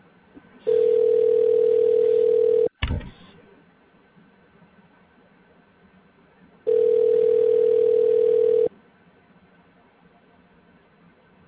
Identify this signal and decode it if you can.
Ringback tone (North America), 440 + 480 Hz.